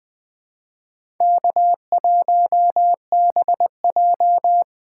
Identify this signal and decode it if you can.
Morse code, heard as a keyed tone, about 20 wpm, 700 Hz. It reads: K1BJ
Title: USA Call Signs - 4 Characters - 4x Character Spacing - 20wpm